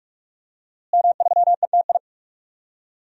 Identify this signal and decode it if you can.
Morse code, heard as a keyed tone, about 46 wpm, 700 Hz. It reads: M3ETI
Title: DXCC-100 Call Signs - 5 Characters - Repeated 3x - 45wpm